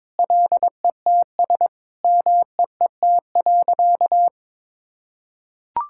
Morse code, heard as a keyed tone, about 22 wpm, 700 Hz. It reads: LETS MEET.
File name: practice / 22wpm / Sentences with 3-Word Phrases - 22wpm